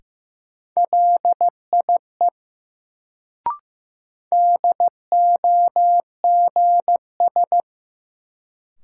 Morse code, heard as a keyed tone, about 15 wpm, 700 Hz. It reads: LIE DOGS
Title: Sentences from Top 1600 Words - New Word Drill - 15wpm